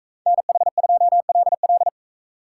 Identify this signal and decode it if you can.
Morse code, heard as a keyed tone, about 41 wpm, 700 Hz. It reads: NH2LL